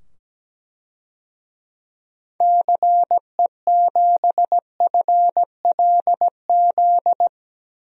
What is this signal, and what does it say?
Morse code, heard as a keyed tone, about 17 wpm, 700 Hz. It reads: CE7FLZ
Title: DXCC-100 Call Signs - 6 Characters - 17wpm